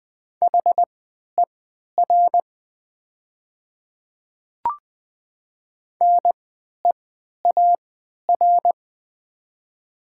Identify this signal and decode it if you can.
Morse code, heard as a keyed tone, about 20 wpm, spaced as for 11 wpm, 700 Hz. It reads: HER NEAR
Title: 3x Character Spacing - Top 200 Words - 20wpm